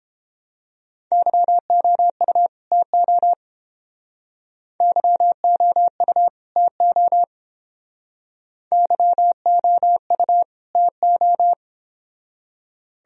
Morse code, heard as a keyed tone, about 30 wpm, 700 Hz. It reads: YOU TO YOU TO YOU TO